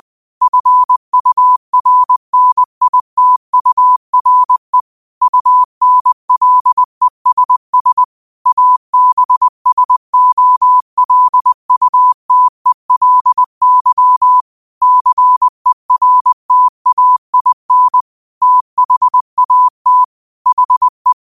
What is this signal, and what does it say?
Morse code, heard as a keyed tone, about 20 wpm, 1000 Hz. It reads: FURNITURE UNLESS ABSOLUTELY CERTAIN THAT HE
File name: qotdfiles / 20wpm